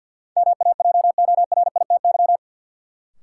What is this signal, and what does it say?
Morse code, heard as a keyed tone, about 48 wpm, 700 Hz. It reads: MAJORITY
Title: Sets of 3 Words - New Word Drill - 50wpm